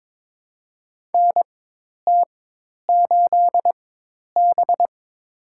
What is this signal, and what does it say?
Morse code, heard as a keyed tone, about 23 wpm, spaced as for 10 wpm, 700 Hz. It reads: NT8B
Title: USA Call Signs - 4x Character Spacing - 22wpm